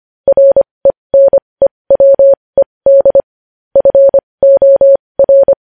Morse code, heard as a keyed tone, about 25 wpm, spaced as for 20 wpm, 550 Hz. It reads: RENEWED FOR
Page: news-headlines-in-morse-code-at-20-wpm